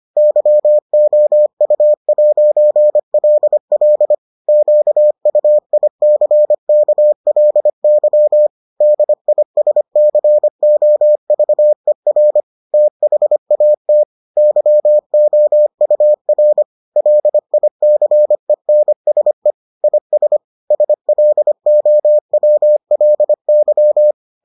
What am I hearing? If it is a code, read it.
Morse code, heard as a keyed tone, about 25 wpm, 600 Hz. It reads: YOU'LL QUICKLY DISCOVER THAT YOUR LICENSE IS SLOWLY